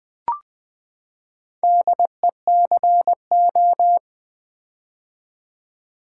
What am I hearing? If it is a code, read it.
Morse code, heard as a keyed tone, about 20 wpm, 700 Hz. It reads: DECO
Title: Build A Word - 4 Letters At A Time - 10-Letter Words - 20wpm